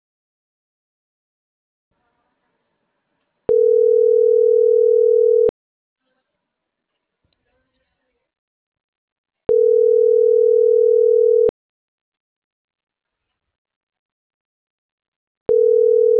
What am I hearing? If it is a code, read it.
Ringback tone (North America), 440 + 480 Hz.